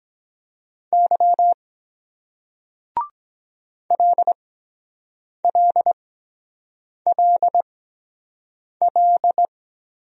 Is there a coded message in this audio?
Yes: Morse code from 0.9 to 9.5 s, heard as a keyed tone, about 23 wpm, spaced as for 7 wpm, 700 Hz. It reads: Y LLLL